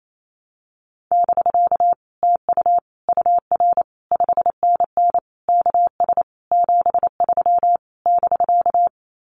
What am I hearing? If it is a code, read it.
Morse code, heard as a keyed tone, about 28 wpm, 700 Hz. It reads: <BK> TU UR 5NN KS 73 <BK>
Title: Common QSO Phrases Encore 28wpm